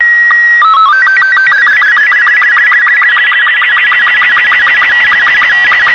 SSTV (slow-scan television), Robot 36.